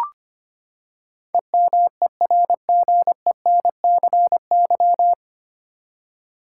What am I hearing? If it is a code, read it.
Morse code, heard as a keyed tone, about 25 wpm, 700 Hz. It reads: EMERGENCY
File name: Nine Letter Words - 25wpm